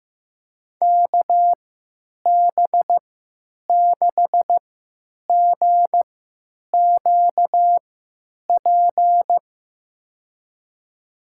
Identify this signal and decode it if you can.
Morse code, heard as a keyed tone, about 16 wpm, spaced as for 9 wpm, 700 Hz. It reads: KB6GQP
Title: USA Call Signs - 6 Characters - 3x Character Spacing - 15wpm